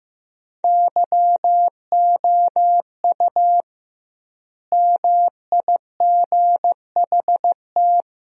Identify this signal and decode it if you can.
Morse code, heard as a keyed tone, about 15 wpm, 700 Hz. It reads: YOU MIGHT
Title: 2x Word-Spacing - Sentences from Top 300 Words - 15wpm